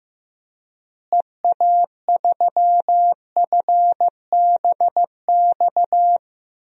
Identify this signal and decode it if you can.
Morse code, heard as a keyed tone, about 15 wpm, 700 Hz. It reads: EA3FBX
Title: DXCC-100 Call Signs - 6 Characters - Repeated 2x - 15wpm